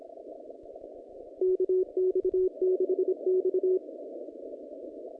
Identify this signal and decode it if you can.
Morse code, heard as a keyed tone, about 26 wpm, 370 Hz. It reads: KX6X
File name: contests / naqp-cw-k5tr-op-k5pi- -0100